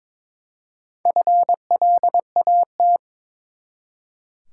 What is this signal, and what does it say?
Morse code, heard as a keyed tone, about 22 wpm, 700 Hz. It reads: FLAT
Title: Four Letter Words - 22wpm